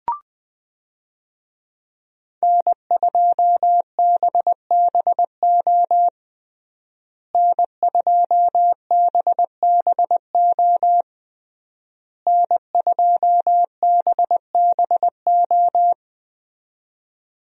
Morse code, heard as a keyed tone, about 20 wpm, 700 Hz. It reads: N2BBO N2BBO N2BBO
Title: USA Call Signs - 5 Characters - Repeated 3x - 20wpm